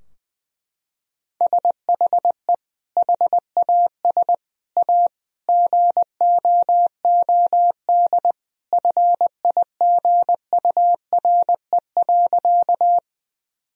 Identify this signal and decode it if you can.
Morse code, heard as a keyed tone, about 21 wpm, 700 Hz. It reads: SHE HAS A GOOD FIGURE.
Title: Sentences from Top 1000 Words - Review Part 2 - 20wpm